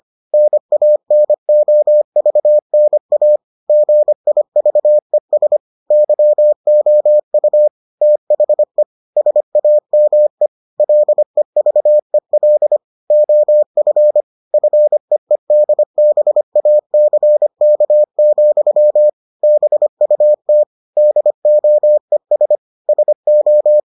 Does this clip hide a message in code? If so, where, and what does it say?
Yes: Morse code from 0.3 to 23.8 s, heard as a keyed tone, about 25 wpm, 600 Hz. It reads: NANOVNA GIVES YOU THE SAME LEVEL OF FEEDBACK, BUT DOES SO